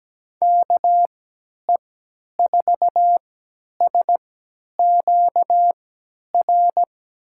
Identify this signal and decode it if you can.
Morse code, heard as a keyed tone, about 17 wpm, spaced as for 10 wpm, 700 Hz. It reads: KE4SQR
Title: USA Call Signs - 6 Characters - 3x Character Spacing - 17wpm